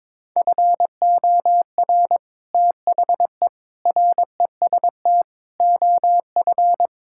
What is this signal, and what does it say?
Morse code, heard as a keyed tone, about 22 wpm, 700 Hz. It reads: FOR THE REST OF
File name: practice / 22wpm / Build A Phrase - 5-Word Phrases - 22wpm